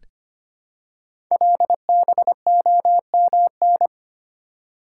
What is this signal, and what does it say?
Morse code, heard as a keyed tone, about 25 wpm, 700 Hz. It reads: LBOMN
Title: Code Groups - 5 Letters - 25wpm